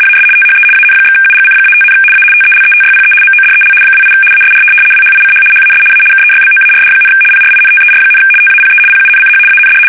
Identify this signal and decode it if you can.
RTTY, about 75 baud. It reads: AS THE AN/UGC-5, A TELETYPE MODEL 28 ASR WHICH HAS A KEYBOARD, PRINTER, TAPE PUNCH AND
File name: fsk850-75B / teletype-corp